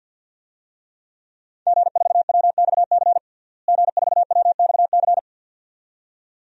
Morse code, heard as a keyed tone, about 52 wpm, 700 Hz. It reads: G4WXC G4WXC
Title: DXCC-100 Call Signs - 5 Characters - Repeated 2x - 50wpm